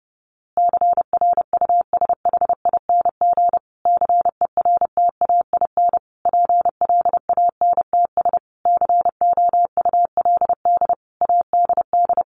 Morse code, heard as a keyed tone, about 30 wpm, 700 Hz. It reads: CRUSHING CERTAIN PLANTS COULD ADD
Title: A Sound Of Thunder 30wpm